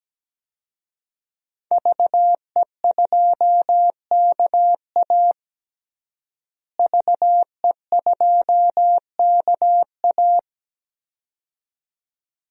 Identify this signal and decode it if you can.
Morse code, heard as a keyed tone, about 17 wpm, 700 Hz. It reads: VE2KA VE2KA
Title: DXCC-100 Call Signs - 5 Characters - Repeated 2x - 17wpm